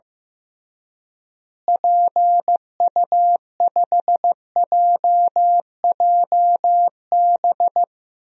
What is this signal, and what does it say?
Morse code, heard as a keyed tone, about 16 wpm, 700 Hz. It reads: PU5JJB
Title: DXCC-100 Call Signs - 6 Characters - Repeated 3x - 15wpm